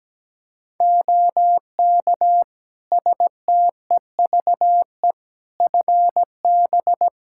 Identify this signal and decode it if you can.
Morse code, heard as a keyed tone, about 17 wpm, 700 Hz. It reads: OK STEVE FB
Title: Practice QSO - Exchange 2 - Info - 17wpm